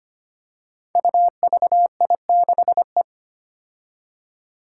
Morse code, heard as a keyed tone, about 25 wpm, 700 Hz. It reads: UVI6E
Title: Code Groups - 5 Letter:Numbers - 25wpm